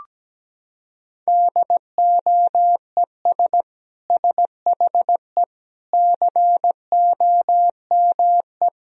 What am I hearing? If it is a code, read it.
Morse code, heard as a keyed tone, about 17 wpm, 700 Hz. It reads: DOES SHE COME
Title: Sentences from Top 1000 Words - Review Part 4 - 17wpm